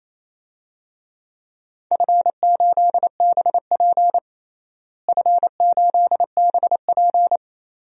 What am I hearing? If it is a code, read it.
Morse code, heard as a keyed tone, about 28 wpm, 700 Hz. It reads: F8BP F8BP
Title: DXCC-100 Call Signs - 4 Characters - Repeated 2x - 28wpm